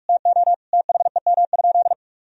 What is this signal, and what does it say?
Morse code, heard as a keyed tone, about 42 wpm, 700 Hz. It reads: TO THEM?
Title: Sentences from Top 1000 Words - Review Part 4 - 45wpm